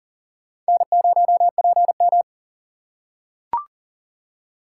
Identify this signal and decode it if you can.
Morse code, heard as a keyed tone, about 40 wpm, 700 Hz. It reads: N0PM
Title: USA Call Signs - 4 Characters - 2x Character Spacing - 40wpm